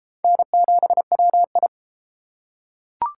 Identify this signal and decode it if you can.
Morse code, heard as a keyed tone, about 32 wpm, 700 Hz. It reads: N7WI